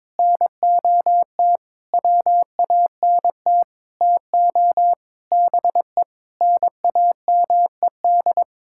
Morse code, heard as a keyed tone, about 22 wpm, 700 Hz. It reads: NOT WANT TO BE NAMED
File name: Sentences with 7-Word Phrases - 22wpm